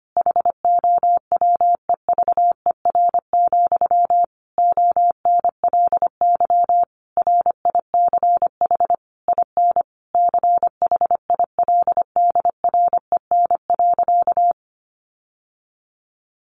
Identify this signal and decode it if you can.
Morse code, heard as a keyed tone, about 25 wpm, 700 Hz. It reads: HOWEVER, ONLY RICH IN CHILDREN.
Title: Sentences from Top 1100 Words 25wpm